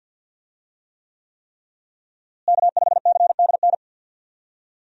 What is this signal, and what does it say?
Morse code, heard as a keyed tone, about 51 wpm, 700 Hz. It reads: K5CDN
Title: USA Call Signs - 5 Characters - Repeated 3x - 50wpm